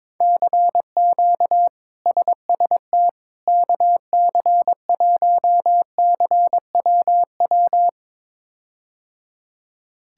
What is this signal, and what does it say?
Morse code, heard as a keyed tone, about 22 wpm, 700 Hz. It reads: CQ SST KC1CWW
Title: K1USN SST QSO Practice - 22wpm